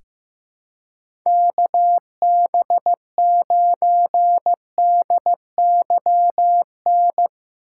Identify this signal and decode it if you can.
Morse code, heard as a keyed tone, about 15 wpm, 700 Hz. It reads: KB9DYN
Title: USA Call Signs - 4x Character Spacing - 15wpm